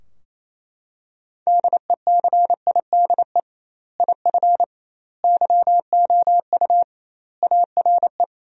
Morse code, heard as a keyed tone, about 28 wpm, 700 Hz. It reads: DECIDE IF YOU ARE
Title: 2x Word-Spacing - Sentences from Top 500 Words - 28wpm